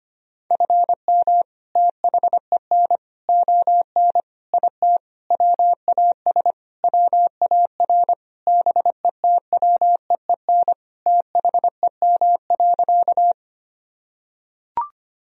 Morse code, heard as a keyed tone, about 25 wpm, 700 Hz. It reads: FM THEN ON IT WAS WAR BETWEEN THEM.
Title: Sentences from Top 400 Words and Abbreviations 25wpm